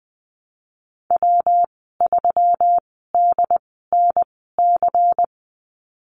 Morse code, heard as a keyed tone, about 19 wpm, spaced as for 14 wpm, 700 Hz. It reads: W3DNC
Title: 147 Lesson 24g - Letter y - Call Signs 20wpm